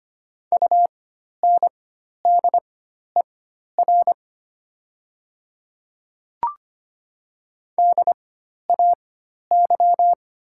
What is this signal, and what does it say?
Morse code, heard as a keyed tone, about 25 wpm, spaced as for 12 wpm, 700 Hz. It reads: UNDER DAY
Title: 4x Character Spacing - Top 500 Words - 25wpm